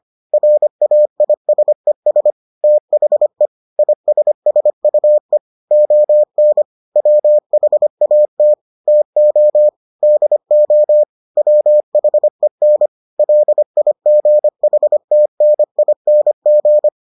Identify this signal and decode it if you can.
Morse code, heard as a keyed tone, about 25 wpm, 600 Hz. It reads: RAISES THE ISSUE ON WHAT TO DO WHEN LIGHTNING